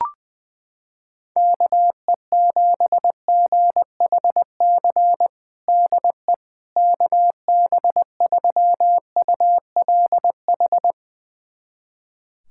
Morse code, heard as a keyed tone, about 20 wpm, 700 Hz. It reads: KE7GHC DE KB3ULH
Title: USA Callsign DE Callsign - 6 Characters - 20wpm